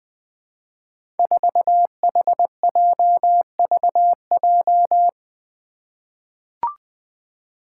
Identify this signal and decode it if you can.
Morse code, heard as a keyed tone, about 20 wpm, 700 Hz. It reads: 4HJVJ
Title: Five Challenging Letter-Number Combinations 20wpm